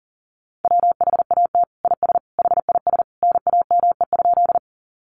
Morse code, heard as a keyed tone, about 40 wpm, 700 Hz. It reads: WHAT IS HIS NAME?
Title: Sentences from Top 200 Words 40wpm